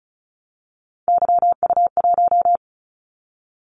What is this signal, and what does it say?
Morse code, heard as a keyed tone, about 36 wpm, 700 Hz. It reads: YU1
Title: Callsign Prefixes for Top 25 DXCC - 35wpm